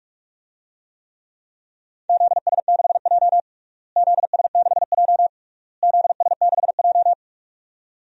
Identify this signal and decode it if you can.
Morse code, heard as a keyed tone, about 46 wpm, 700 Hz. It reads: ZS6J ZS6J ZS6J
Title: DXCC-100 Call Signs - 4 Characters - Repeated 3x - 45wpm